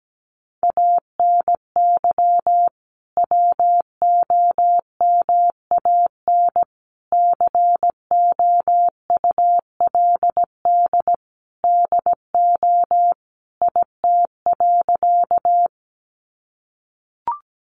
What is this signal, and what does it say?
Morse code, heard as a keyed tone, about 17 wpm, 700 Hz. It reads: ANY WOMAN COULD DO IT.
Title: Sentences from Top 1100 Words 17wpm